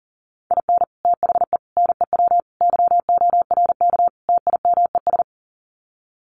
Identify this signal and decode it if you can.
Morse code, heard as a keyed tone, about 37 wpm, 700 Hz. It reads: IN THE NEW YORK TIMES
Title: Sets of 5 Words 40wpm